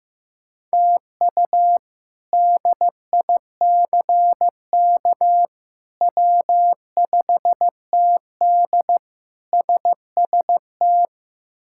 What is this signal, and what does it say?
Morse code, heard as a keyed tone, about 15 wpm, 700 Hz. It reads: TU DICK W5TD SST
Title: K1USN SST QSO Practice - 15wpm